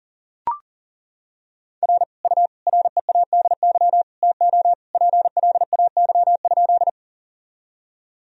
Morse code, heard as a keyed tone, about 40 wpm, 700 Hz. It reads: R U READY TO PLAY?
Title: Sentences from Top 400 Words and Abbreviations 40wpm